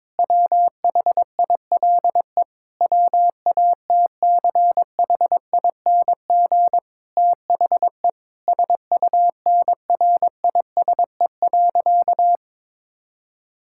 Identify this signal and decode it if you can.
Morse code, heard as a keyed tone, about 22 wpm, 700 Hz. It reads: WHILE WATCHING THE SUNRISE.
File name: Difficult Sentences with Hints - 22wpm